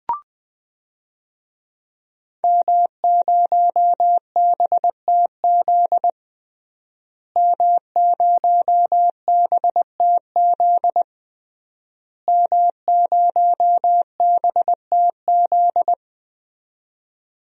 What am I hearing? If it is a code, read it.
Morse code, heard as a keyed tone, about 20 wpm, 700 Hz. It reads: M0BTZ M0BTZ M0BTZ
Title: DXCC-100 Call Signs - 5 Characters - Repeated 3x - 20wpm